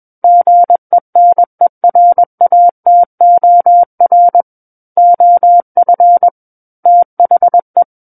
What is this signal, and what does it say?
Morse code, heard as a keyed tone, about 21 wpm, 700 Hz. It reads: GENERATOR OF THE
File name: Book One / 12-Chapter 12 - 21 WPM